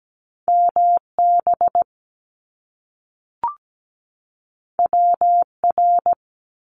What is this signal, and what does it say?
Morse code, heard as a keyed tone, about 17 wpm, 700 Hz. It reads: MB WR
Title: Top Two Letter Combinations in Words 17wpm